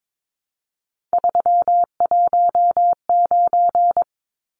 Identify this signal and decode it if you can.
Morse code, heard as a keyed tone, about 22 wpm, 700 Hz. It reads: 319
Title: Numbers - 3-Digit 22wpm